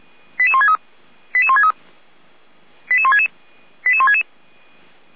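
Five-tone selective call (ZVEI), heard as quick runs of five tones, tones of about 70 ms. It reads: 88163 88163 88166 88166
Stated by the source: Five-tone ZVEI selective call received on SDR, fire brigade Orlamünde (feuerwehr-orlamuende).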